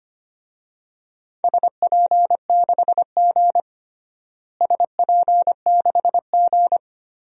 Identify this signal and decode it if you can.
Morse code, heard as a keyed tone, about 25 wpm, 700 Hz. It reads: SP6G SP6G
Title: DXCC-100 Call Signs - 4 Characters - Repeated 3x - 25wpm